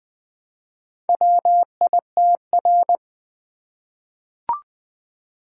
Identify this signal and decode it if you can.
Morse code, heard as a keyed tone, about 20 wpm, 700 Hz. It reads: WITR
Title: Four Letter-Number Combinations - 20wpm